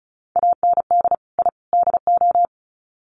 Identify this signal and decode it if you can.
Morse code, heard as a keyed tone, about 34 wpm, 700 Hz. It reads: AND I DO
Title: Sets of 3 Words 35wpm